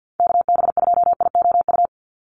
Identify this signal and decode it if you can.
Morse code, heard as a keyed tone, about 50 wpm, 700 Hz. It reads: KB2IOU
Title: USA Call Signs 50wpm